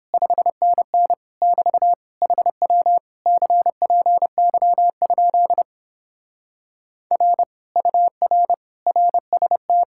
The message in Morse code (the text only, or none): 5NN <BT> HW CPY? R UR RST